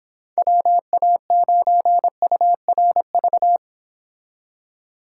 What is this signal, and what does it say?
Morse code, heard as a keyed tone, about 29 wpm, 700 Hz. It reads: WA9URV